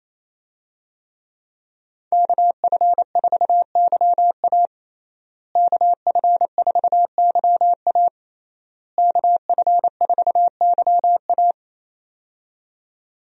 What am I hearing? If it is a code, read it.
Morse code, heard as a keyed tone, about 28 wpm, 700 Hz. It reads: KF4YA KF4YA KF4YA
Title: USA Call Signs - 5 Characters - Repeated 3x - 28wpm